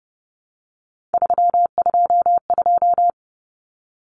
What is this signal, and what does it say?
Morse code, heard as a keyed tone, about 30 wpm, 700 Hz. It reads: 322
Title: Numbers - 3-Digit 30wpm